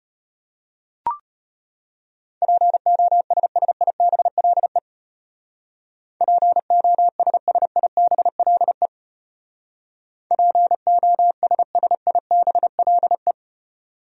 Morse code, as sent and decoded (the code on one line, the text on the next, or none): .--. --- ... ... .. -... .-.. / .--. --- ... ... .. -... .-.. / .--. --- ... ... .. -... .-.. .
POSSIBLE POSSIBLE POSSIBLE